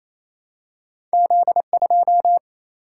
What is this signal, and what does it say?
Morse code, heard as a keyed tone, about 28 wpm, 700 Hz. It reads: Z 2
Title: Single-Letter Single-Number 28wpm